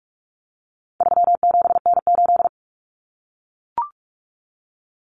Morse code, heard as a keyed tone, about 44 wpm, 700 Hz. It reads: 37N8
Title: Common Cut Numbers - 4-Digit 45wpm